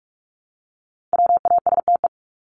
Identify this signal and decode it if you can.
Morse code, heard as a keyed tone, about 44 wpm, 700 Hz. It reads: WASTE
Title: Top 1751-2000 Words 45wpm